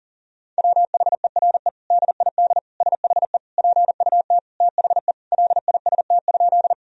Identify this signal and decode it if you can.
Morse code, heard as a keyed tone, about 40 wpm, 700 Hz. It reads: WHERE DID SHE PUT THE LIST?
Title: Sentences from Top 1000 Words - Review Part 2 - 40wpm